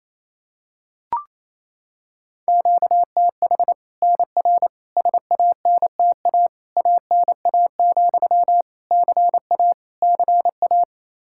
Morse code, heard as a keyed tone, about 27 wpm, 700 Hz. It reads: QTH NR SANTA ANA, CA CA
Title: Practice QSO - Exchange 1 - Beginning - 28wpm